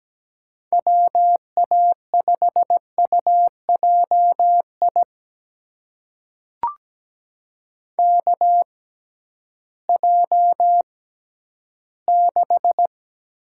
Morse code, heard as a keyed tone, about 17 wpm, spaced as for 11 wpm, 700 Hz. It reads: WA5UJI KJ6
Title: USA Call Signs - 6 Characters - 6x Character Spacing - 17wpm